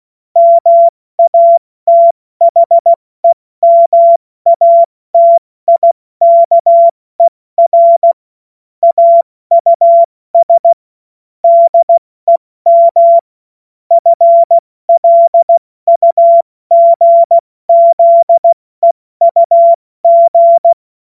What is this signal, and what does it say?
Morse code, heard as a keyed tone, about 16 wpm, 680 Hz. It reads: MATHEMATIKER AUS DEM FLUGZEUG
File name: Morsekurs / Witze / Witz 07. Dezember 2021, 70 F80